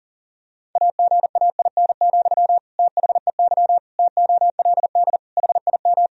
Morse code, heard as a keyed tone, about 40 wpm, 700 Hz. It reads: AGAIN, THEY TOLD HIM
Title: Sentences from Top 1000 Words - Review Part 3 - 40wpm